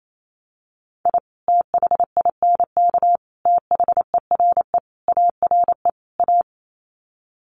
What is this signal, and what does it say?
Morse code, heard as a keyed tone, about 28 wpm, 700 Hz. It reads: I THINK THERE ARE A
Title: Sets of 5 Words Encore2 28wpm